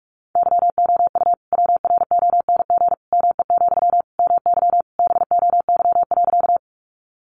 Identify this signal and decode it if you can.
Morse code, heard as a keyed tone, about 44 wpm, 700 Hz. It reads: YOU WRONG ME, MY BOY.
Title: Sentences from Top 900 Words 45wpm